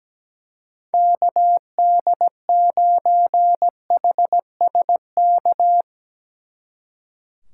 Morse code, heard as a keyed tone, about 18 wpm, 700 Hz. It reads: KD9HSK
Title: Speed Racing - USA 2x3 Call Signs 17wpm